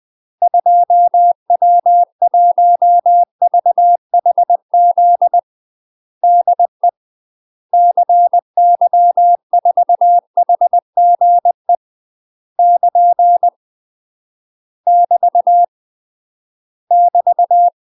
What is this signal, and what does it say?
Morse code, heard as a keyed tone, about 20 wpm, 700 Hz. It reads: 2W1VHZ DE CY4HGE <KN> <BT> <BT>
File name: UK MASTER QSOs TRAINING / 20WPM / UK MASTER QSOs 0064